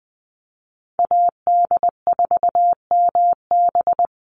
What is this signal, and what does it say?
Morse code, heard as a keyed tone, about 20 wpm, 700 Hz. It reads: AD4MB
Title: USA Call Signs 20wpm